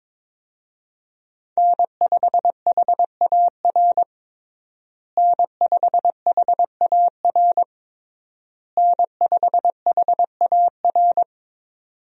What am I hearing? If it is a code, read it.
Morse code, heard as a keyed tone, about 22 wpm, 700 Hz. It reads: N5HAR N5HAR N5HAR
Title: USA Call Signs - 5 Characters - Repeated 3x - 22wpm